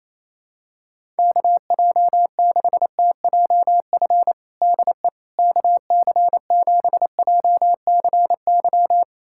Morse code, heard as a keyed tone, about 28 wpm, 700 Hz. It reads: KJ6TJF DE KC7JCY
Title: USA Callsign DE Callsign - 6 Characters - 28wpm